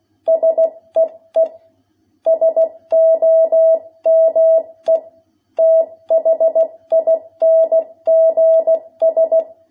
Morse code, heard as a keyed tone, about 16 wpm, 660 Hz. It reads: SEE SOME THINGS
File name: jh8cbh / 2021cw